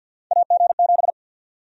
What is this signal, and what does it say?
Morse code, heard as a keyed tone, about 51 wpm, 700 Hz. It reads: AG7
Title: USA Call Sign Prefixes - Repeated 2x - 50wpm